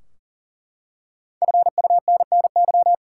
Morse code, heard as a keyed tone, about 40 wpm, 700 Hz. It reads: FUNNY
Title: Five Letter Words - 40wpm